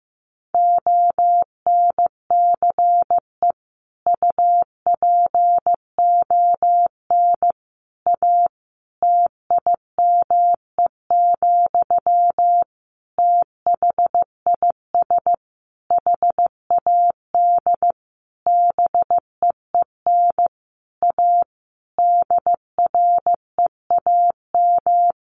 Morse code, heard as a keyed tone, about 15 wpm, 700 Hz. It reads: ONCE UPON A TIME, THIS HAD BEEN A DREAM
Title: Sentences from Top 1500 Words 15wpm